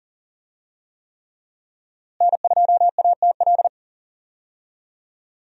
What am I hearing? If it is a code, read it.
Morse code, heard as a keyed tone, about 37 wpm, 700 Hz. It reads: N2ATL
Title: USA Call Signs - 5 Characters - Repeated 3x - 40wpm